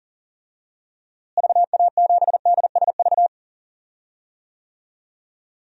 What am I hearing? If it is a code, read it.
Morse code, heard as a keyed tone, about 39 wpm, 700 Hz. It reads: VA7DSV
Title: DXCC-100 Call Signs - 6 Characters - Repeated 3x - 40wpm